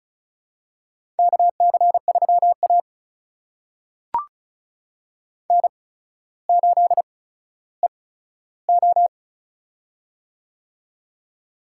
Morse code, heard as a keyed tone, about 36 wpm, spaced as for 15 wpm, 700 Hz. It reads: KC3A N8EO
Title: USA Call Signs - 8x Character Spacing - 35wpm